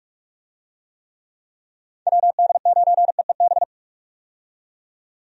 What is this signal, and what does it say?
Morse code, heard as a keyed tone, about 45 wpm, 700 Hz. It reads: WD9EEB
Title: USA Call Signs - 6 Characters - 45wpm